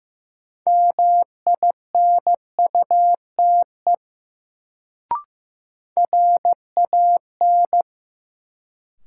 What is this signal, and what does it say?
Morse code, heard as a keyed tone, about 15 wpm, 700 Hz. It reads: MINUTE RAN
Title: Sentences from Top 500 Words - New Word Drill - 15wpm